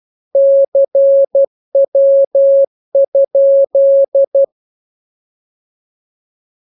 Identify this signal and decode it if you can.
Morse code, heard as a keyed tone, about 12 wpm, 550 Hz. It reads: CW?
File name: jokes / joke 14 12 0000